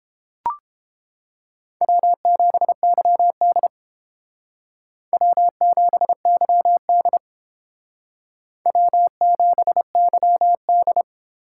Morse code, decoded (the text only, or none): W7YD W7YD W7YD